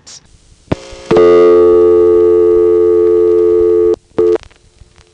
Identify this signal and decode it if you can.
Dial tone (North America), 350 + 440 Hz.
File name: phonetrips / Greenville1xxp1HQ